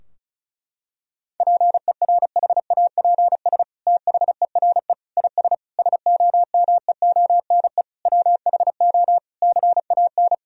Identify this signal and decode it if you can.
Morse code, heard as a keyed tone, about 36 wpm, 700 Hz. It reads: PERHAPS THERE IS SOMEONE WHO CAN
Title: Sentences from Top 1000 Words - Review Part 4 - 35wpm